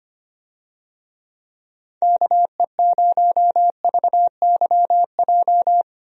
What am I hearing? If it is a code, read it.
Morse code, heard as a keyed tone, about 25 wpm, 700 Hz. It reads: KE0VYJ
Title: USA Call Signs - 6 Characters - 25wpm